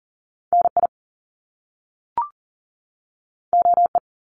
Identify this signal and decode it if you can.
Morse code, heard as a keyed tone, about 40 wpm, 700 Hz. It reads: NI OE
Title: Top Two Letter Combinations in Words 40wpm